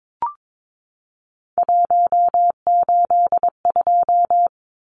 Morse code, heard as a keyed tone, about 22 wpm, 700 Hz. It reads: 182
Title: Numbers - 3-Digit 22wpm